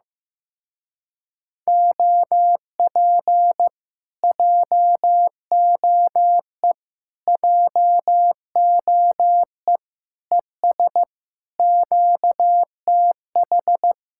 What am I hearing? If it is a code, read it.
Morse code, heard as a keyed tone, about 15 wpm, 700 Hz. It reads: OP JOE JOE ES QTH
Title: Speed-Racing Common QSO Phrases Encore - 15wpm